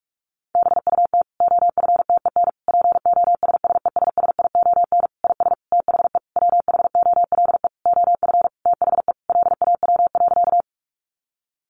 Morse code, heard as a keyed tone, about 44 wpm, 700 Hz. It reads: BUT OFTEN POSSESSION IS THE WHOLE OF THE LAW.
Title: Sentences from Top 1400 Words 45wpm